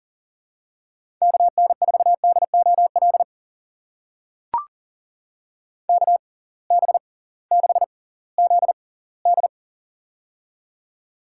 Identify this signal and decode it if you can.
Morse code, heard as a keyed tone, about 38 wpm, spaced as for 21 wpm, 700 Hz. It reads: KN4DOL KB6ZD
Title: USA Call Signs - 6x Character Spacing - 40wpm